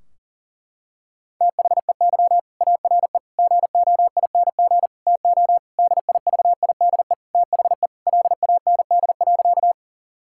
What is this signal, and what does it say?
Morse code, heard as a keyed tone, about 40 wpm, 700 Hz. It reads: THEY ARE GOING TO DIVIDE THE LAND.
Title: Sentences from Top 1600 Words 40wpm